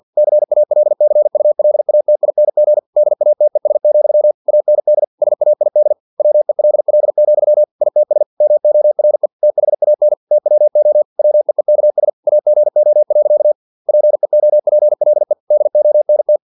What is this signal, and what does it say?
Morse code, heard as a keyed tone, about 50 wpm, 600 Hz. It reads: CALCULATING DATES, AND SAID WELL, ITS MORE THAN TWO WEEKS AGO. PEOPLE DONT
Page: Never 49wpm